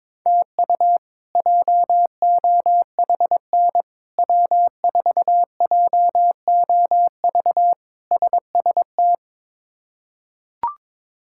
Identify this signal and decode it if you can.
Morse code, heard as a keyed tone, about 22 wpm, 700 Hz. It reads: TU JOHN W4JOV SST
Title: K1USN SST QSO Practice - 22wpm